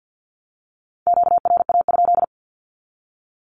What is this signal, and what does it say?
Morse code, heard as a keyed tone, about 48 wpm, 700 Hz. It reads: QRA?
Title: Q-Codes 50wpm